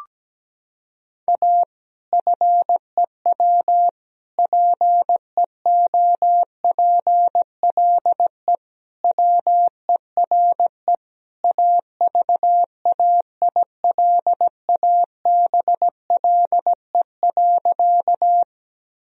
Morse code, heard as a keyed tone, about 17 wpm, 700 Hz. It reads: A FEW PEOPLE WERE AVAILABLE.
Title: Sentences from Top 1000 Words - Review Part 3 - 17wpm